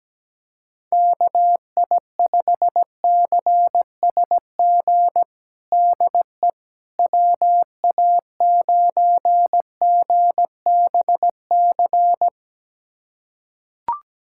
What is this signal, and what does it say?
Morse code, heard as a keyed tone, about 17 wpm, 700 Hz. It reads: KI5CSG DE WA9GBC
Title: USA Callsign DE Callsign - 6 Characters - 17wpm